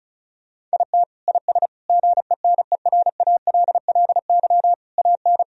Morse code, heard as a keyed tone, about 36 wpm, 700 Hz. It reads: IT IS GENERALLY AN
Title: Sentences from Top 1000 Words - Review Part 4 - 35wpm